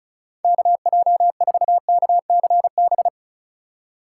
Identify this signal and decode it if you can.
Morse code, heard as a keyed tone, about 36 wpm, 700 Hz. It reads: KJ4KCB
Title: USA Call Signs - 6 Characters - 3x Character Spacing - 35wpm